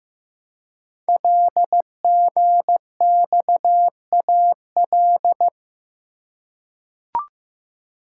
Morse code, heard as a keyed tone, about 15 wpm, 700 Hz. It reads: LGXAL
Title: Code Groups - 5 Letters - 15wpm